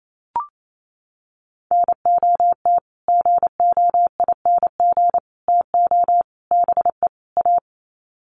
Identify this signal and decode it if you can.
Morse code, heard as a keyed tone, about 28 wpm, 700 Hz. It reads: NOT GOING TO BE A
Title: Sets of 5 Words Encore 28wpm